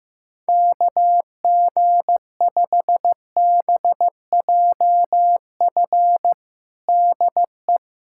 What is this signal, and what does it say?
Morse code, heard as a keyed tone, about 15 wpm, 700 Hz. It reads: KG5BJF DE
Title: USA Callsign DE Callsign - 6 Characters - 15wpm